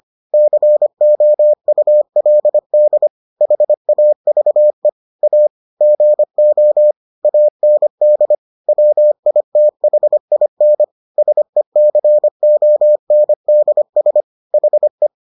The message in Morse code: COULD HAVE A GO AND WITHIN SECONDS HE

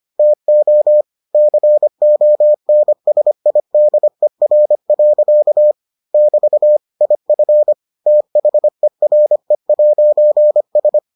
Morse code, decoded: TO CONSIDER. <BT> IF THERE'S

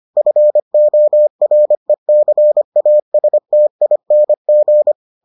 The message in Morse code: FORECASTING

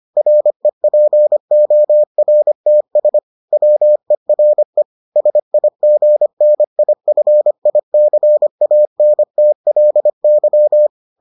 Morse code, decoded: REPORTS WERE SIGNIFICANTLY

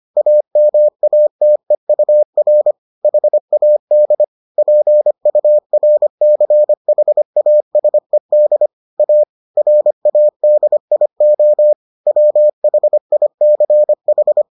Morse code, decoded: AMATEUR HAD PURCHASED A RADIO WHICH

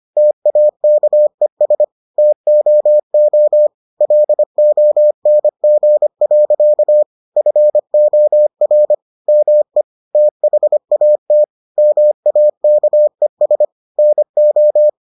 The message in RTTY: none

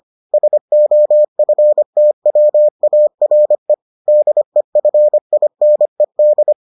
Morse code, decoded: SOFTWARE DEFINED

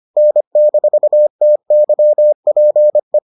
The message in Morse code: N-TYPE

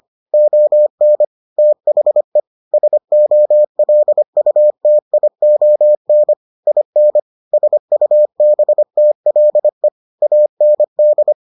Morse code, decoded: ON THE SOLUTION IN SUBTLE AND

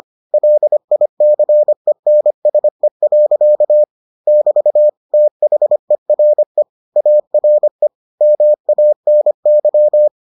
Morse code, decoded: LICENSE. <BT> THERE ARE MANY

